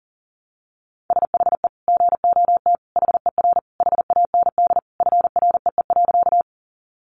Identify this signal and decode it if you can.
Morse code, heard as a keyed tone, about 40 wpm, 700 Hz. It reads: SHE GOT HER HAND FREE.
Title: Sentences from Top 500 Words 40wpm